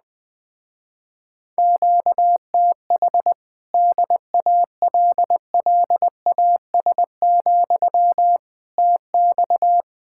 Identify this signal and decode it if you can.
Morse code, heard as a keyed tone, about 20 wpm, 700 Hz. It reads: QTH DALLAS, TX
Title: Speed-Racing Common QSO Phrases Encore - 17wpm